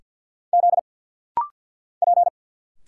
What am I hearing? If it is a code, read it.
Morse code, heard as a keyed tone, about 50 wpm, 690 Hz. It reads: Z P
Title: Single Letters - Rapid-Fire 50wpm